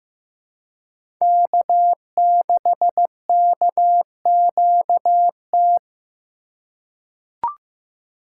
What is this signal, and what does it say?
Morse code, heard as a keyed tone, about 15 wpm, 700 Hz. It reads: K6KQT
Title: USA Call Signs - 5 Characters - 3x Character Spacing - 15wpm